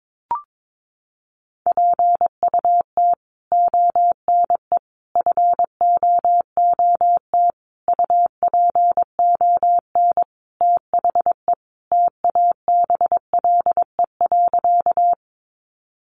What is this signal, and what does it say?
Morse code, heard as a keyed tone, about 22 wpm, 700 Hz. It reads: PUT ONE FOOT UPON THE TABLE.
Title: Sentences from Top 1100 Words 22wpm